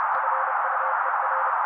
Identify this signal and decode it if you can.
Morse code, heard as a keyed tone, about 30 wpm, 530 Hz. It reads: FUER
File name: daily / 29wpm noise